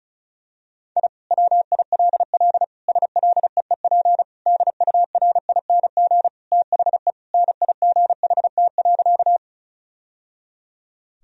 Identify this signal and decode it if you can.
Morse code, heard as a keyed tone, about 36 wpm, 700 Hz. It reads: I WILL SLEEP DURING THE NIGHT.
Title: Sentences from Top 1000 Words - Review Part 2 - 35wpm